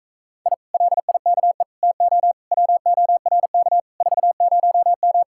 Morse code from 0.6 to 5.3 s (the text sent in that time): LIKE TO WORK 40M